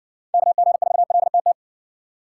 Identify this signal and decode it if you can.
Morse code, heard as a keyed tone, about 60 wpm, 700 Hz. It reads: KG4LTT